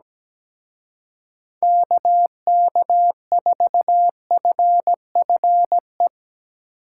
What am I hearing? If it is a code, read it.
Morse code, heard as a keyed tone, about 17 wpm, 700 Hz. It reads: KK4FFE
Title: Speed Racing - USA 2x3 Call Signs 17wpm